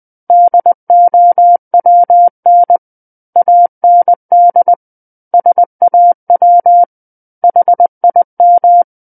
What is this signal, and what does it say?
Morse code, heard as a keyed tone, about 20 wpm, 700 Hz. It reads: DOWN AND SAW HIM
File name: 20 WPM